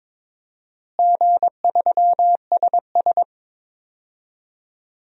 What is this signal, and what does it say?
Morse code, heard as a keyed tone, about 22 wpm, 700 Hz. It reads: G3SS
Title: DXCC-100 Call Signs - 4 Characters - Repeated 3x - 22wpm